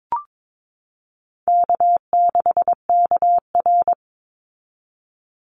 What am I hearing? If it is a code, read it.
Morse code, heard as a keyed tone, about 22 wpm, 700 Hz. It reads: K6KR
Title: Easy Call Signs Repeated 2x - 22wpm